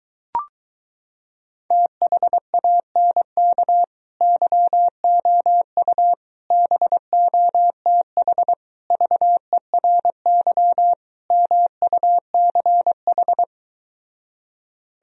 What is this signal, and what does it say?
Morse code, heard as a keyed tone, about 23 wpm, 700 Hz. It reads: THANK YOU BOTH VERY MUCH